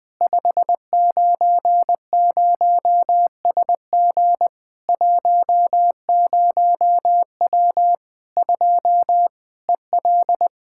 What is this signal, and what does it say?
Morse code, heard as a keyed tone, about 20 wpm, 700 Hz. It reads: 590SG 10W 2 EL
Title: Practice QSO - Exchange 2 - Info - 20wpm